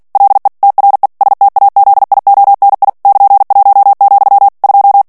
Morse code, heard as a keyed tone, about 48 wpm, 800 Hz. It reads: LE TRE STAZIONI <KN>1, 2
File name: ITALIEN texte clair 2 vitesse 45 mots minute